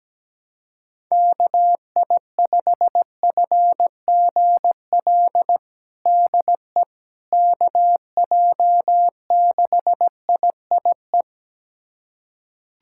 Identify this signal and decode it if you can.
Morse code, heard as a keyed tone, about 17 wpm, 700 Hz. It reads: KI5FGL DE KJ6IIE
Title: USA Callsign DE Callsign - 6 Characters - 17wpm